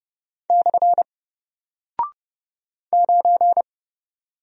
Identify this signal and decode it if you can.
Morse code, heard as a keyed tone, about 30 wpm, 700 Hz. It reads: / 9
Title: Single Character (Letter:Number:Punctuation:Prosign) Alt 30wpm